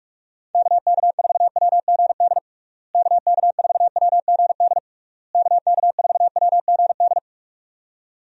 Morse code, heard as a keyed tone, about 46 wpm, 700 Hz. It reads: KK4WGD KK4WGD KK4WGD
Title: USA Call Signs - 6 Characters - Repeated 3x - 45wpm